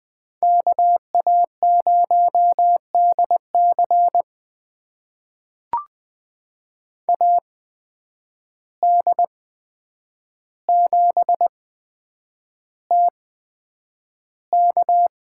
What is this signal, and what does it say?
Morse code, heard as a keyed tone, about 20 wpm, spaced as for 9 wpm, 700 Hz. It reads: KA0DC AD7TK
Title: USA Call Signs - 5 Characters - 8x Character Spacing - 20wpm